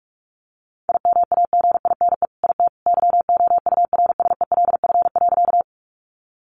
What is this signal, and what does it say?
Morse code, heard as a keyed tone, about 44 wpm, 700 Hz. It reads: IMAGINE IT YOURSELF.
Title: Sentences from Top 1400 Words 45wpm